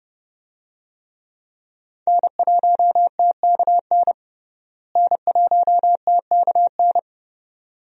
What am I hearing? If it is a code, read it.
Morse code, heard as a keyed tone, about 32 wpm, 700 Hz. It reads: N1TKN N1TKN
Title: USA Call Signs - 5 Characters - Repeated 2x - 30wpm